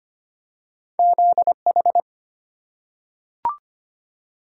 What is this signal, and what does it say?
Morse code, heard as a keyed tone, about 25 wpm, 690 Hz. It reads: ZH
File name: Two Letter-Number Combinations - 25wpm